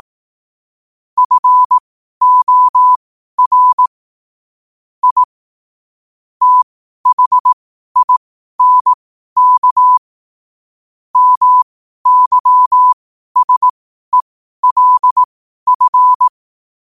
Morse code, heard as a keyed tone, about 18 wpm, spaced as for 12 wpm, 1000 Hz. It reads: FOR I THINK MYSELF
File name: qotdfiles / 12wpm